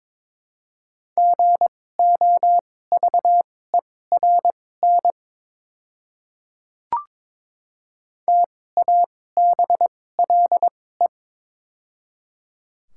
Morse code, heard as a keyed tone, about 22 wpm, spaced as for 16 wpm, 700 Hz. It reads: GOVERN TABLE
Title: 2x Character Spacing - Top 500 Words - 22wpm